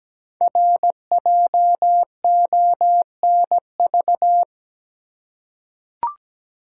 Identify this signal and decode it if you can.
Morse code, heard as a keyed tone, about 17 wpm, 700 Hz. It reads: RJONV
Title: Code Groups - 5 Letters - 17wpm